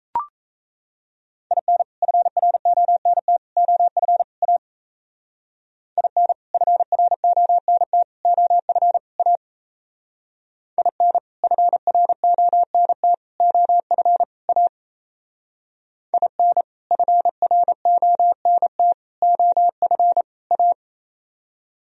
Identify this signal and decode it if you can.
Morse code, heard as a keyed tone, about 36 wpm, 690 Hz. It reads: IN FRONT OF A IN FRONT OF A IN FRONT OF A IN FRONT OF A